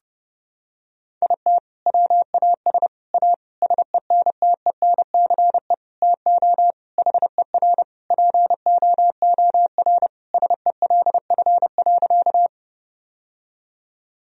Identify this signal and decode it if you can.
Morse code, heard as a keyed tone, about 30 wpm, 700 Hz. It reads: IT WAS A SENTENCE TO HER POOR SELF.
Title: Sentences from Top 1000 Words - Review Part 1 - 30wpm